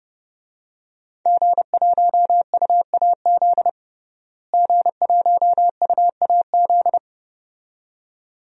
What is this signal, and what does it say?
Morse code, heard as a keyed tone, about 30 wpm, 700 Hz. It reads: G1UAZ G1UAZ
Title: DXCC-100 Call Signs - 5 Characters - Repeated 2x - 30wpm